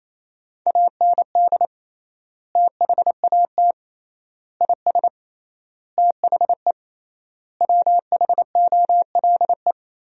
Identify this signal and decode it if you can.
Morse code, heard as a keyed tone, about 28 wpm, 700 Hz. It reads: AND THAT IS THE WHOLE
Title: 3x Word-Spacing - Sentences from Top 400 Words - 28wpm